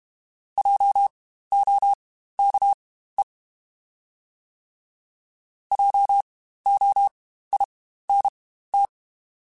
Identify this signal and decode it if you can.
Morse code, heard as a keyed tone, about 32 wpm, spaced as for 15 wpm, 800 Hz. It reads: JOKE JOINT
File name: Spaced Alphas / Spaced Alphas mp3s / Js sp @30